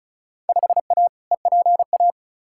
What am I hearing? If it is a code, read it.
Morse code, heard as a keyed tone, about 34 wpm, 700 Hz. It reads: 5A EPA
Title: Field Day Exchanges - 35wpm